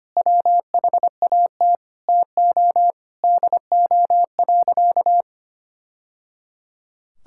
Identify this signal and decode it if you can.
Morse code, heard as a keyed tone, about 25 wpm, 700 Hz. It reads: WHAT TO DO.
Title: Sentences from Top 1000 Words - Review Part 5 - 25wpm